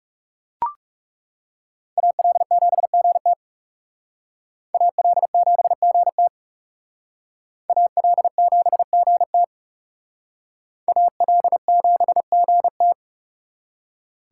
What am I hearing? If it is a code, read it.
Morse code, heard as a keyed tone, about 38 wpm, 700 Hz. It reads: AL7GT AL7GT AL7GT AL7GT